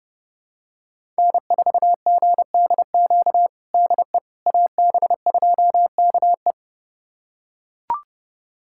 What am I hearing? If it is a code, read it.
Morse code, heard as a keyed tone, about 30 wpm, 700 Hz. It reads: N4GDQ DE AB2KE
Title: USA Callsign DE Callsign - 5 Characters - 30wpm